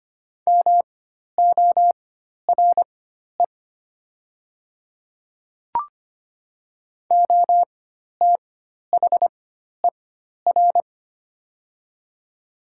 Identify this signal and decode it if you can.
Morse code, heard as a keyed tone, about 25 wpm, spaced as for 12 wpm, 690 Hz. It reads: MORE OTHER
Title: 4x Character Spacing - Top 100 Words - 25wpm